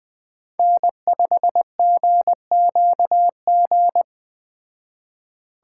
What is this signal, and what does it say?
Morse code, heard as a keyed tone, about 20 wpm, 700 Hz. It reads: N5GQG
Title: Speed Racing - USA 1x3 Call Signs 17wpm